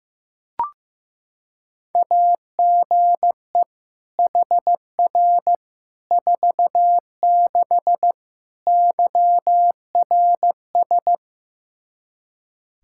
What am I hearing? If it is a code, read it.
Morse code, heard as a keyed tone, about 15 wpm, 700 Hz. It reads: AGE HR 46 YRS
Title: K1USN SST QSO Practice 15wpm